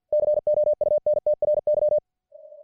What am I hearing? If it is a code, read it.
Morse code, heard as a keyed tone, about 53 wpm, 600 Hz. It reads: COUNTRY